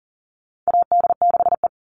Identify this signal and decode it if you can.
Morse code, heard as a keyed tone, about 40 wpm, 700 Hz. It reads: AD6E
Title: Call Signs Repeated 3x - 40wpm